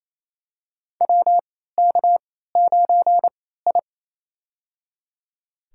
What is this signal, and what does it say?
Morse code, heard as a keyed tone, about 27 wpm, spaced as for 16 wpm, 700 Hz. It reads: WK9I
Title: USA Call Signs - 4 Characters - 3x Character Spacing - 28wpm